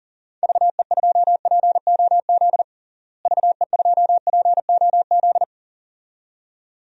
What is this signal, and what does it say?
Morse code, heard as a keyed tone, about 40 wpm, 700 Hz. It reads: VE2POZ VE2POZ
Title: DXCC-100 Call Signs - 6 Characters - Repeated 2x - 40wpm